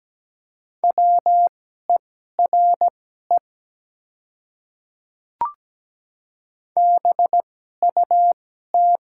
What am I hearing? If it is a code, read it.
Morse code, heard as a keyed tone, about 17 wpm, spaced as for 12 wpm, 700 Hz. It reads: WERE BUT
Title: 2x Character Spacing - Top 100 Words - 17wpm